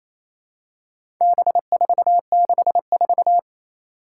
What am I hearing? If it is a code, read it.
Morse code, heard as a keyed tone, about 27 wpm, 700 Hz. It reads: B464
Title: Four Challenging Letter-Number Combinations 28wpm